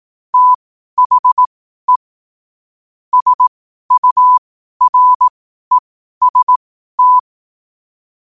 Morse code, heard as a keyed tone, about 18 wpm, spaced as for 12 wpm, 1000 Hz. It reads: THE SUREST
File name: qotdfiles / 12wpm